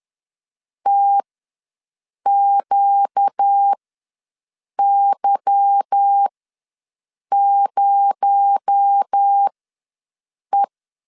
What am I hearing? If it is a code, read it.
Morse code, heard as a keyed tone, about 10 wpm, spaced as for 6 wpm, 800 Hz. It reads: TQY0E
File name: LEZIONI / 8 YJ45 / 8 YJ45 ws 10wpm 3843